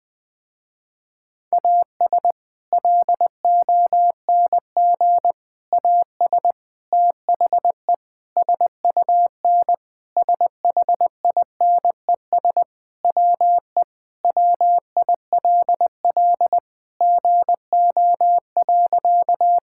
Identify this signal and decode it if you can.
Morse code, heard as a keyed tone, about 20 wpm, 700 Hz. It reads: AS LONG AS THE SUN SHINES WE WILL GO.
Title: Sentences with 4-Word Phrases - Spoken Beginning - 20wpm